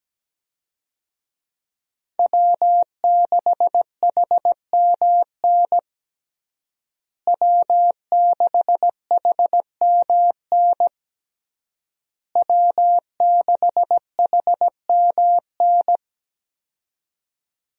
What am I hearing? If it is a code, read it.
Morse code, heard as a keyed tone, about 17 wpm, 700 Hz. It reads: W6HMN W6HMN W6HMN
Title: USA Call Signs - 5 Characters - Repeated 3x - 17wpm